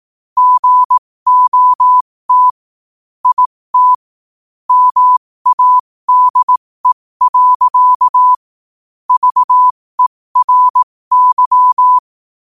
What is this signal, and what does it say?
Morse code, heard as a keyed tone, about 18 wpm, spaced as for 15 wpm, 1000 Hz. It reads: GOT IT MADE. VERY
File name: qotdfiles / 15wpm